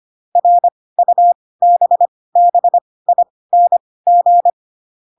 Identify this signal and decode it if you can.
Morse code, heard as a keyed tone, about 25 wpm, spaced as for 18 wpm, 700 Hz. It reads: RUBBING
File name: Pinocchio / Pinocchio ch1 12 25 ik1hge